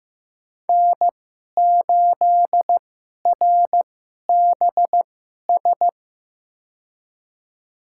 Morse code, heard as a keyed tone, about 15 wpm, spaced as for 11 wpm, 700 Hz. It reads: N8RBS
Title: USA Call Signs - 5 Characters - 2x Character Spacing - 15wpm